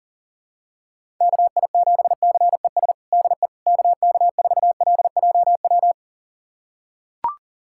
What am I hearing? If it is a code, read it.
Morse code, heard as a keyed tone, about 40 wpm, 700 Hz. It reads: KI7CES DE KK4LJW
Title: USA Callsign DE Callsign 40wpm